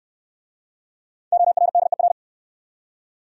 Morse code, heard as a keyed tone, about 63 wpm, 700 Hz. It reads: ORDER